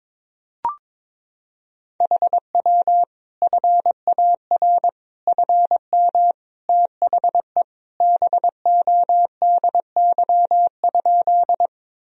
Morse code, heard as a keyed tone, about 22 wpm, 700 Hz. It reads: HW FAR FM THE BODY?
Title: Sentences from Top 400 Words and Abbreviations 22wpm